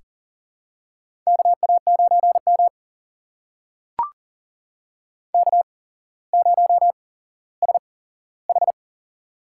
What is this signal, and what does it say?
Morse code, heard as a keyed tone, about 37 wpm, spaced as for 17 wpm, 700 Hz. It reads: KA9M K0SH
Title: USA Call Signs - 4 Characters - 8x Character Spacing - 40wpm